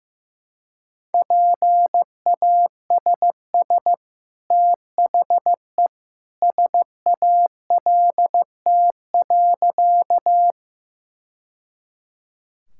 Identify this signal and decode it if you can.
Morse code, heard as a keyed tone, about 15 wpm, 700 Hz. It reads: PASS THE SALT.
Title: Sentences from Top 1000 Words - Review Part 5 - 15wpm